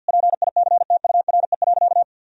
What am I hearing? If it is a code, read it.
Morse code, heard as a keyed tone, about 50 wpm, 700 Hz. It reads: PICTURE.
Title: Sentences from Top 1000 Words - Review Part 4 - 50wpm